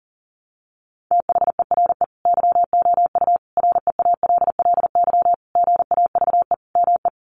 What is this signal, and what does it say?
Morse code, heard as a keyed tone, about 40 wpm, 700 Hz. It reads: THERE YOU REALLY GAVE ME
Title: Sentences from Top 900 Words 40wpm